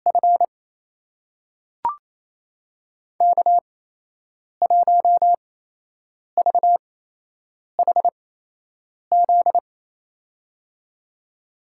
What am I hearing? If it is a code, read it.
Morse code, heard as a keyed tone, about 28 wpm, spaced as for 8 wpm, 700 Hz. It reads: F K1VHZ
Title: USA Call Signs - 5 Characters - 8x Character Spacing - 28wpm